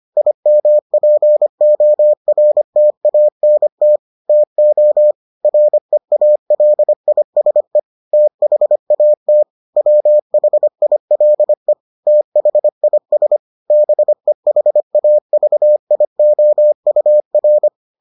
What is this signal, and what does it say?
Morse code, heard as a keyed tone, about 25 wpm, 600 Hz. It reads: IMPORTANT TO REALISE THAT WHILE THIS BEHAVIOUR